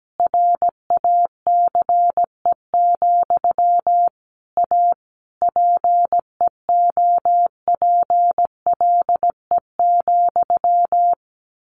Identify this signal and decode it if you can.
Morse code, heard as a keyed tone, about 17 wpm, 700 Hz. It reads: RACE, A PEOPLE,
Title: A Sound Of Thunder 17wpm